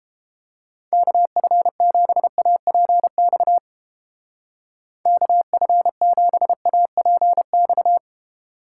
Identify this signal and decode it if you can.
Morse code, heard as a keyed tone, about 29 wpm, 700 Hz. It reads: KF7APX KF7APX